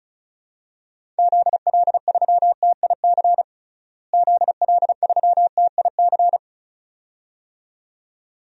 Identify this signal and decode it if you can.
Morse code, heard as a keyed tone, about 34 wpm, 700 Hz. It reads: ZL3TIC ZL3TIC
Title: DXCC-100 Call Signs - 6 Characters - Repeated 2x - 35wpm